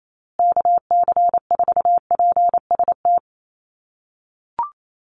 Morse code, heard as a keyed tone, about 28 wpm, 700 Hz. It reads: KC4PST
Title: USA Call Signs Repeated 3x 28wpm